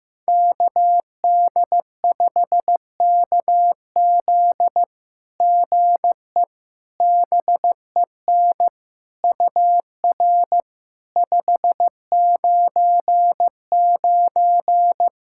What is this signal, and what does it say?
Morse code, heard as a keyed tone, about 15 wpm, 700 Hz. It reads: KD5KZ GE BEN UR 599
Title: SOTA QSO Practice 15wpm